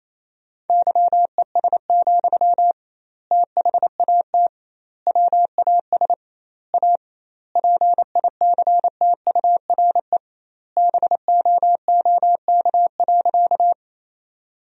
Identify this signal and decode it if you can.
Morse code, heard as a keyed tone, about 28 wpm, 700 Hz. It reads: YES, THAT WAS A PICTURE BOOK.
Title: 2x Word-Spacing - Sentences from Top 500 Words - 28wpm